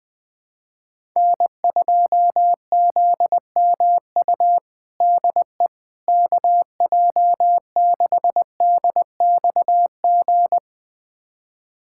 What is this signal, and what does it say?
Morse code, heard as a keyed tone, about 20 wpm, 700 Hz. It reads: N2ZMU DE KJ6DXG
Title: USA Callsign DE Callsign 20wpm